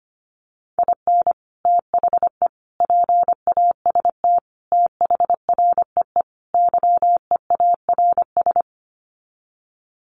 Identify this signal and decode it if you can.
Morse code, heard as a keyed tone, about 25 wpm, 700 Hz. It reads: IN THE PAST THREE YEARS
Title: Sets of 5 Words Encore2 25wpm